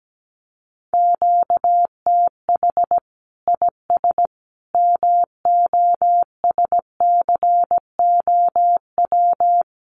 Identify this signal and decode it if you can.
Morse code, heard as a keyed tone, about 17 wpm, 700 Hz. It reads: QTH IS MOSCOW
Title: Common QSO Phrases 17wpm